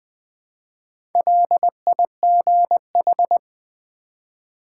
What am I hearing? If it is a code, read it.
Morse code, heard as a keyed tone, about 20 wpm, 700 Hz. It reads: LIGH
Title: Build A Word - 1 Letter At A Time - 5-Letter Words - 20wpm